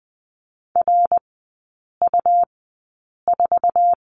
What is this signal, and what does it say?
Morse code, heard as a keyed tone, about 20 wpm, spaced as for 8 wpm, 700 Hz. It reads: RU4
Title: 100 Lesson 17e - Letter u - Three Characters 20wpm